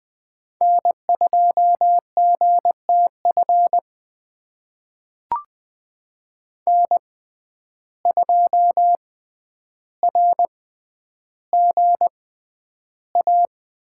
Morse code, heard as a keyed tone, about 20 wpm, spaced as for 10 wpm, 700 Hz. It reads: N2GTF N2RGA
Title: USA Call Signs - 5 Characters - 6x Character Spacing - 20wpm